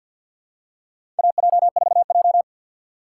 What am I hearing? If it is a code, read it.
Morse code, heard as a keyed tone, about 47 wpm, 700 Hz. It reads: AJ4J
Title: USA Call Signs - 4 Characters - 2x Character Spacing - 50wpm